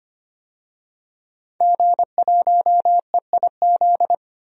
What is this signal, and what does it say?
Morse code, heard as a keyed tone, about 25 wpm, 700 Hz. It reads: G1EIZ
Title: DXCC-100 Call Signs - 5 Characters - 25wpm